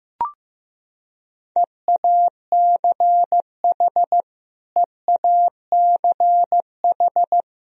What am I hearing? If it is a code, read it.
Morse code, heard as a keyed tone, about 15 wpm, 700 Hz. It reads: EACH EACH
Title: Top 100 Words Repeated 15wpm